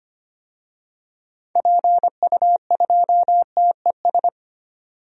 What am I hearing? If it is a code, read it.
Morse code, heard as a keyed tone, about 25 wpm, 700 Hz. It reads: PU2TES
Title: DXCC-100 Call Signs - 6 Characters - 25wpm